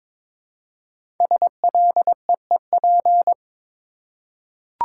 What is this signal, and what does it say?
Morse code, heard as a keyed tone, about 22 wpm, 700 Hz. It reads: SLEEP
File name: 5 Letter Words - First 3 Letters Given - 22wpm